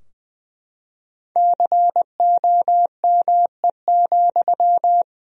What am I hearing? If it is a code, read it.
Morse code, heard as a keyed tone, about 20 wpm, 700 Hz. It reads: COME,
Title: Sentences from Top 1000 Words - Review Part 4 - 20wpm